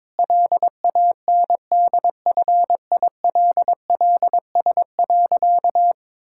Morse code, heard as a keyed tone, about 22 wpm, 700 Hz. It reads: LANDFILLS.
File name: Difficult Sentences with Hints - 22wpm